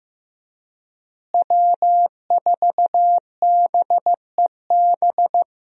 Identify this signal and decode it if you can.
Morse code, heard as a keyed tone, about 16 wpm, 700 Hz. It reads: W4BEB
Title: USA Call Signs - 5 Characters - Repeated 2x - 15wpm